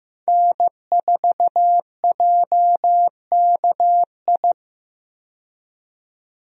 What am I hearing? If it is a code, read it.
Morse code, heard as a keyed tone, about 15 wpm, 700 Hz. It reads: N4JKI
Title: USA Call Signs - 5 Characters - 15wpm